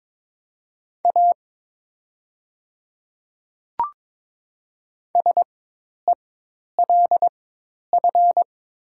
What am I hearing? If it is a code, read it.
Morse code, heard as a keyed tone, about 22 wpm, spaced as for 10 wpm, 700 Hz. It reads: A SELF
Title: 4x Character Spacing - Top 200 Words - 22wpm